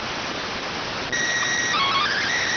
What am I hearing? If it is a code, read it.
SSTV (slow-scan television), PD 290.